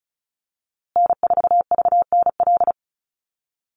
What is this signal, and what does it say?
Morse code, heard as a keyed tone, about 34 wpm, 700 Hz. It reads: N4VNL
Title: USA Call Signs Repeated 3x 35wpm